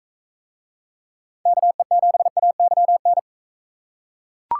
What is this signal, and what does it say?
Morse code, heard as a keyed tone, about 44 wpm, 700 Hz. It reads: KE7AYN